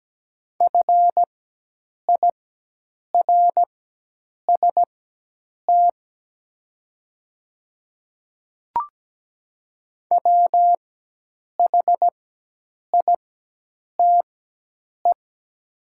Morse code, heard as a keyed tone, about 17 wpm, spaced as for 8 wpm, 700 Hz. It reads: FIRST WHITE
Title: 4x Character Spacing - Top 300 Words - 17wpm